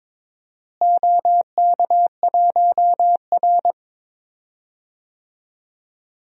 Morse code, heard as a keyed tone, about 22 wpm, 700 Hz. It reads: OK1R
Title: DXCC-100 Call Signs - 4 Characters - Repeated 3x - 22wpm